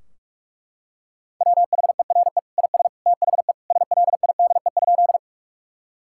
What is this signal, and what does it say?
Morse code, heard as a keyed tone, about 41 wpm, 700 Hz. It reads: WHERE IS THE SLIDE?
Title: Sentences from Top 1600 Words 45wpm